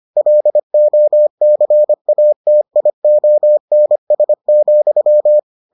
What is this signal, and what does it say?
Morse code, heard as a keyed tone, about 25 wpm, 600 Hz. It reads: LOCATIONS,